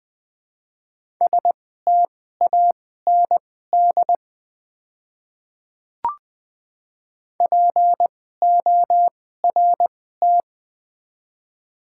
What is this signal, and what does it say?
Morse code, heard as a keyed tone, about 20 wpm, spaced as for 14 wpm, 700 Hz. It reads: STAND PORT
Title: 2x Character Spacing - Top 500 Words - 20wpm